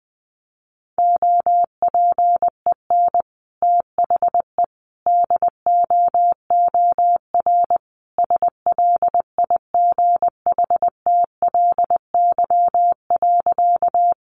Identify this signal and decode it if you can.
Morse code, heard as a keyed tone, about 20 wpm, 700 Hz. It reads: OPEN THE DOOR SLIGHTLY.
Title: Sentences from Top 1300 Words 20wpm